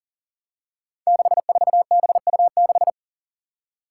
Morse code, heard as a keyed tone, about 40 wpm, 700 Hz. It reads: 64BU6
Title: Five Challenging Letter-Number Combinations 40wpm